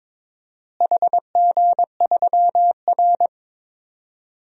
Morse code, heard as a keyed tone, about 22 wpm, 700 Hz. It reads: HG3R
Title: DXCC-100 Call Signs - 4 Characters - Repeated 2x - 22wpm